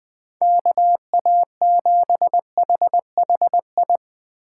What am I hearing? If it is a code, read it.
Morse code, heard as a keyed tone, about 20 wpm, 700 Hz. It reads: KA7HHI